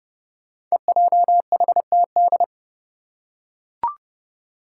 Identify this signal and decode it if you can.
Morse code, heard as a keyed tone, about 30 wpm, 700 Hz. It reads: EJHTD
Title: Five Challenging Letter-Number Combinations 30wpm